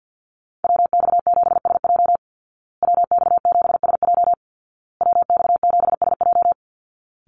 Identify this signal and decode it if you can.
Morse code, heard as a keyed tone, about 50 wpm, 700 Hz. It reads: WX7SJ WX7SJ WX7SJ
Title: Call Signs Repeated 2x - 50wpm